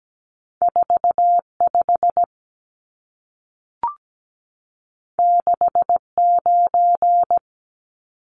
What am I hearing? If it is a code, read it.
Morse code, heard as a keyed tone, about 17 wpm, 700 Hz. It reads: 45 69
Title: Numbers - 2-Digit 17wpm